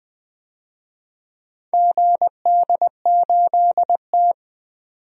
Morse code, heard as a keyed tone, about 20 wpm, 700 Hz. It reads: GD8T
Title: DXCC-100 Call Signs - 4 Characters - 20wpm